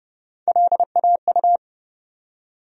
Morse code, heard as a keyed tone, about 30 wpm, 700 Hz. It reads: LAU
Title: USA Call Sign Postfixes - 3 Letters 30wpm